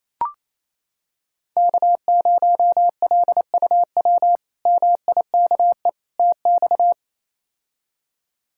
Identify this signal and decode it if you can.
Morse code, heard as a keyed tone, about 28 wpm, 700 Hz. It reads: K0LUW MIKE TX
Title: K1USN SST QSO Practice - 28wpm